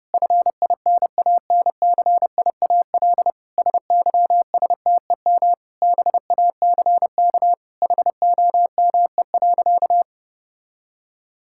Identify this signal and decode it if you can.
Morse code, heard as a keyed tone, about 30 wpm, 700 Hz. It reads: FINANCIAL SYSTEM BACK HOME.
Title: Sentences from Top 1000 Words - Review Part 5 - 30wpm